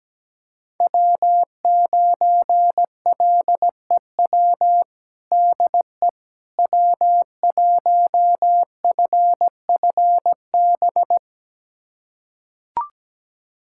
Morse code, heard as a keyed tone, about 17 wpm, 700 Hz. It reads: W9LEW DE W1FFB
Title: USA Callsign DE Callsign - 5 Characters - 17wpm